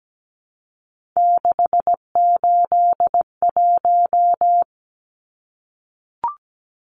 Morse code, heard as a keyed tone, about 17 wpm, 700 Hz. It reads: 681
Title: Numbers - 3-Digit 17wpm